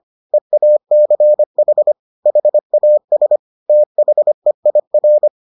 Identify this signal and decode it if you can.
Morse code, heard as a keyed tone, about 25 wpm, 600 Hz. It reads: EACH HAS THEIR